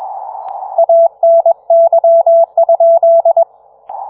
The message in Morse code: ANY?